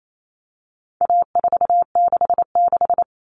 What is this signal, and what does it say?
Morse code, heard as a keyed tone, about 28 wpm, 700 Hz. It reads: A466
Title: Common Cut Numbers - 4-Digit 28wpm